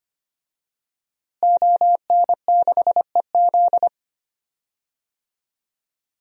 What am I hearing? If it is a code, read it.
Morse code, heard as a keyed tone, about 25 wpm, 700 Hz. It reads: ON6EZ
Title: DXCC-100 Call Signs - 5 Characters - Repeated 3x - 25wpm